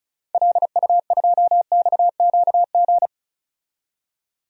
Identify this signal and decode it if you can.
Morse code, heard as a keyed tone, about 36 wpm, 700 Hz. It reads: LU2XQG
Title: DXCC-100 Call Signs - 6 Characters - 35wpm